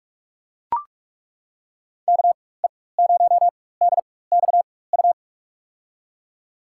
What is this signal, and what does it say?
Morse code, heard as a keyed tone, about 42 wpm, spaced as for 21 wpm, 700 Hz. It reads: KE0DXU
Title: USA Call Signs - 6 Characters - 4x Character Spacing - 45wpm